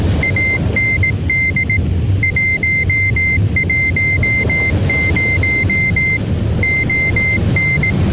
Morse code, heard as a keyed tone, about 18 wpm, 2100 Hz, with noise. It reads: AND 110 ON